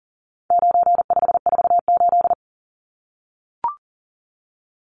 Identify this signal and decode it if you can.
Morse code, heard as a keyed tone, about 40 wpm, 700 Hz. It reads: 9548
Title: Numbers - 4-Digit 40wpm